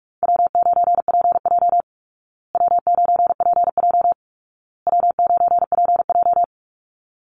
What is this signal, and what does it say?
Morse code, heard as a keyed tone, about 44 wpm, 700 Hz. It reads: W9PJ W9PJ W9PJ
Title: Easy Call Signs Repeated 3x - 45wpm